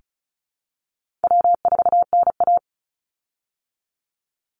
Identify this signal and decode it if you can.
Morse code, heard as a keyed tone, about 34 wpm, 700 Hz. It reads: W4NA
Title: Call Signs Repeated 3x - 35wpm